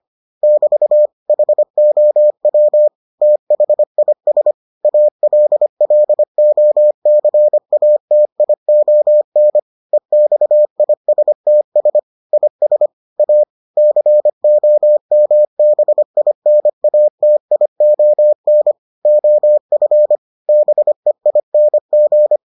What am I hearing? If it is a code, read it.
Morse code, heard as a keyed tone, about 25 wpm, 600 Hz. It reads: <BT> HOW THIS ALLOCATION EXISTS IS A COMBINATION OF BEING